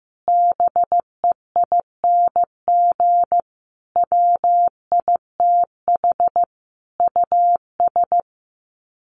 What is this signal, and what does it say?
Morse code, heard as a keyed tone, about 15 wpm, 700 Hz. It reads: BEING WITH US
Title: Sets of 4 Words 15wpm